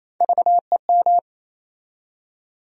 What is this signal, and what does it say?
Morse code, heard as a keyed tone, about 28 wpm, 700 Hz. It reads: VEM
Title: USA Call Sign Postfixes - 3 Letters 28wpm